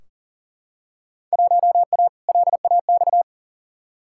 Morse code, heard as a keyed tone, about 40 wpm, 700 Hz. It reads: 1A LAX
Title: Field Day Exchanges - 40wpm